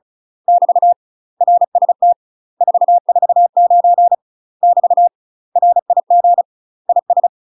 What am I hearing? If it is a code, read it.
Morse code, heard as a keyed tone, about 34 wpm, 700 Hz. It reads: <BT> RST 449 <BT> RIG IS
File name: UK MASTER QSOs TRAINING / 35WPM / UK MASTER QSOs 0042